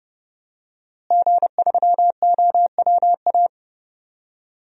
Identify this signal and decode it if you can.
Morse code, heard as a keyed tone, about 30 wpm, 700 Hz. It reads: G3OWA
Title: DXCC-100 Call Signs - 5 Characters - Repeated 3x - 30wpm